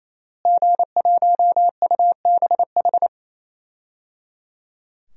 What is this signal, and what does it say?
Morse code, heard as a keyed tone, about 28 wpm, 700 Hz. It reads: G1UBH
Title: DXCC-100 Call Signs - 5 Characters - Repeated 3x - 28wpm